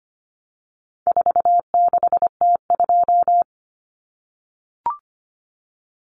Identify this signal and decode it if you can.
Morse code, heard as a keyed tone, about 25 wpm, 700 Hz. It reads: 46T2
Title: Common Cut Numbers - 4-Digit 25wpm